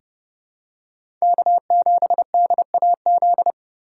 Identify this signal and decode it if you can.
Morse code, heard as a keyed tone, about 30 wpm, 700 Hz. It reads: K7DAZ
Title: USA Call Signs - 5 Characters - 3x Character Spacing - 30wpm